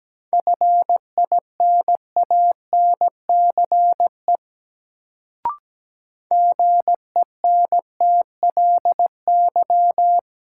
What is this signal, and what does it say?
Morse code, heard as a keyed tone, about 17 wpm, 700 Hz. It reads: FINANCE GENTLY
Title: Sentences from Top 1500 Words - New Word Drill - 17wpm